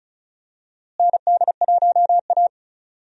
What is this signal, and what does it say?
Morse code, heard as a keyed tone, about 36 wpm, 700 Hz. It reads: ND1A
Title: USA Call Signs - 4 Characters - Repeated 3x - 35wpm